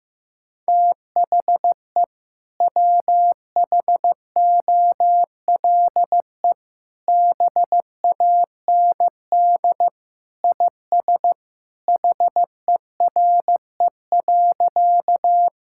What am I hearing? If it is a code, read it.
Morse code, heard as a keyed tone, about 14 wpm, 700 Hz. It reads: THE WHOLE BAND IS HERE.
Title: Sentences from Top 1600 Words 15wpm